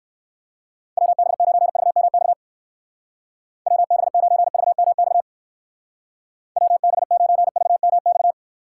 Morse code, heard as a keyed tone, about 54 wpm, 700 Hz. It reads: WB9VMX WB9VMX WB9VMX
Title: Speed Racing - USA 2x3 Call Signs 45wpm